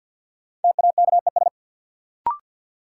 Morse code, heard as a keyed tone, about 53 wpm, 700 Hz. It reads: TAKES
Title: Sets of 5 Words Encore - New Word Drill - 50wpm